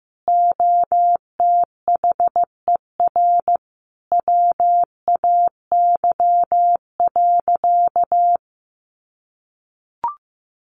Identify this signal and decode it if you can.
Morse code, heard as a keyed tone, about 15 wpm, 700 Hz. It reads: OTHER WAY.
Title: Sentences from Top 100 Words 15wpm